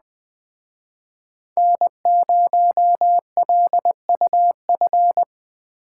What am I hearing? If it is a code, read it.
Morse code, heard as a keyed tone, about 20 wpm, 700 Hz. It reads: N0LUF